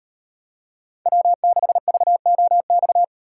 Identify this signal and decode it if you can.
Morse code, heard as a keyed tone, about 38 wpm, 700 Hz. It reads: W6VOX